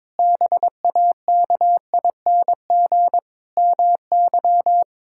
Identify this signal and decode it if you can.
Morse code, heard as a keyed tone, about 22 wpm, 700 Hz. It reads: BAKING MY
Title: Pangram Challenge v2 - 22wpm